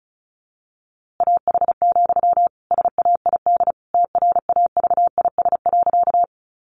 Morse code, heard as a keyed tone, about 34 wpm, 700 Hz. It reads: AH, SAID TRAVIS.
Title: A Sound Of Thunder 35wpm